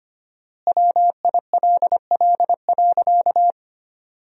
Morse code, heard as a keyed tone, about 25 wpm, 700 Hz. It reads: WILL.
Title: 4x Word-Spacing - Sentences from Top 500 Words - 25wpm